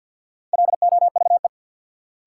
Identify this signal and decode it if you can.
Morse code, heard as a keyed tone, about 51 wpm, 700 Hz. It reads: LOVE
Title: Sets of 5 Words Encore - New Word Drill - 50wpm